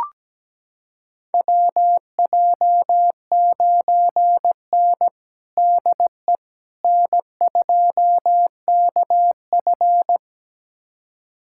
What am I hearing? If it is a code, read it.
Morse code, heard as a keyed tone, about 17 wpm, 700 Hz. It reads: WJ9N DE N2KF
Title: USA Callsign DE Callsign - 4 Characters - 17wpm